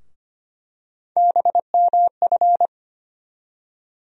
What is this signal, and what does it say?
Morse code, heard as a keyed tone, about 25 wpm, 700 Hz. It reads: BMF
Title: USA Call Sign Postfixes - 3 Letters 25wpm